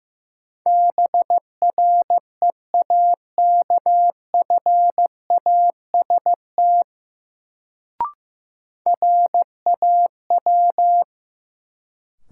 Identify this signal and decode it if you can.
Morse code, heard as a keyed tone, about 15 wpm, 700 Hz. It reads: BREAKFAST RAW
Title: Sentences from Top 1500 Words - New Word Drill Alt - 15wpm